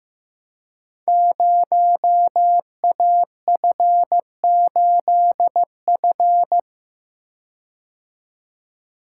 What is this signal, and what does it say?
Morse code, heard as a keyed tone, about 15 wpm, 700 Hz. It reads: 0AF8F
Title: Code Groups - 5 Letter:Numbers - 15wpm